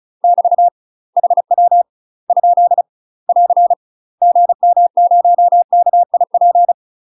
Morse code, heard as a keyed tone, about 34 wpm, 700 Hz. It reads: <BT> HW ? <AR> GM0KIP
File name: UK MASTER QSOs TRAINING / 35WPM / UK MASTER QSOs 0030